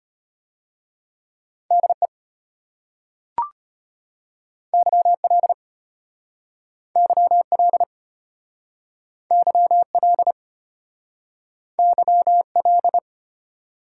Morse code, heard as a keyed tone, about 32 wpm, 700 Hz. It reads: DE YL YL YL YL